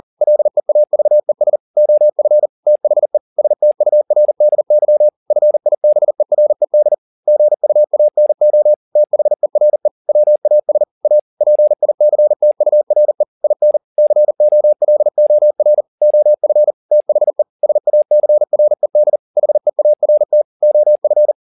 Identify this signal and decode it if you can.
Morse code, heard as a keyed tone, about 40 wpm, 600 Hz. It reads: LEAVES OF THE STURDY FIBERED GUANO THERE WAS A PICTURE IN COLOR OF THE SACRED HEART OF